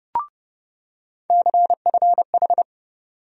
Morse code, heard as a keyed tone, about 30 wpm, 700 Hz. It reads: CFH
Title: USA Call Sign Postfixes - 3 Letters 30wpm